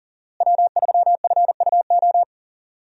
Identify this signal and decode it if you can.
Morse code, heard as a keyed tone, about 39 wpm, 700 Hz. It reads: W3FUO
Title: USA Call Signs - 5 Characters - Repeated 3x - 40wpm